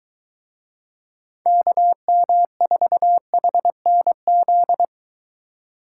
Morse code, heard as a keyed tone, about 23 wpm, 700 Hz. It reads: KM4HNZ